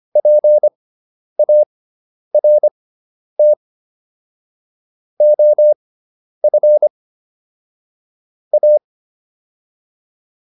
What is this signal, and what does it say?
Morse code, heard as a keyed tone, about 25 wpm, spaced as for 10 wpm, 600 Hz. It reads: PART OF A